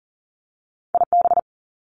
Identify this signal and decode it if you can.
Morse code, heard as a keyed tone, about 38 wpm, 700 Hz. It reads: IB
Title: Top Two Letter Combinations in Words 40wpm